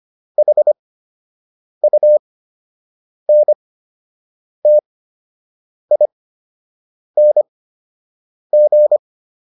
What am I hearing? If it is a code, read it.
Morse code, heard as a keyed tone, about 25 wpm, spaced as for 7 wpm, 600 Hz. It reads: HUNTING